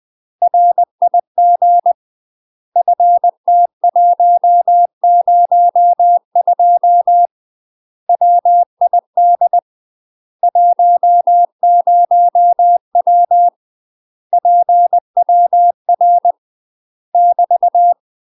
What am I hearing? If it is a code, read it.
Morse code, heard as a keyed tone, about 20 wpm, 700 Hz. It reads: RIG FT102 WID 10W PWR <BT>
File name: UK MASTER QSOs TRAINING / 20WPM / UK MASTER QSOs 0017